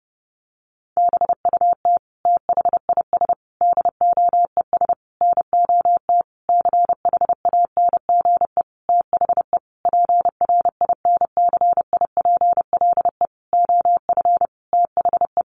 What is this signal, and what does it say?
Morse code, heard as a keyed tone, about 30 wpm, 700 Hz. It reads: BUT THIS DOES NOT CHANGE THE PRINCIPLE OF THE
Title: Sentences from Top 1400 Words 30wpm